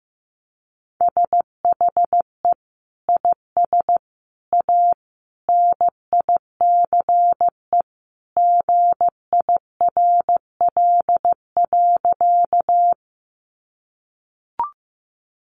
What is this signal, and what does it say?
Morse code, heard as a keyed tone, about 15 wpm, 700 Hz. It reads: SHE IS A NICE GIRL.
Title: Sentences from Top 1100 Words 15wpm